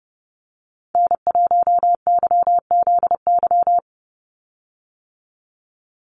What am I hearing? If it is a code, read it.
Morse code, heard as a keyed tone, about 30 wpm, 700 Hz. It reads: N1YZY
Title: USA Call Signs 30wpm